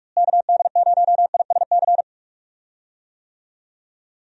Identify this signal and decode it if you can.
Morse code, heard as a keyed tone, about 46 wpm, 700 Hz. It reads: KD0ISC
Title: USA Call Signs - 6 Characters - Repeated 3x - 45wpm